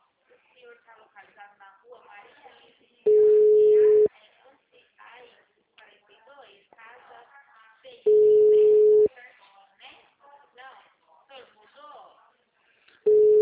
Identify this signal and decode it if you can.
Ringback tone (Europe, 425 Hz), with speech.